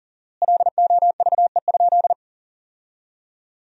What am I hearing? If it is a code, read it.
Morse code, heard as a keyed tone, about 40 wpm, 700 Hz. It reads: LOVE?
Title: 4x Word-Spacing - Sentences from Top 500 Words - 40wpm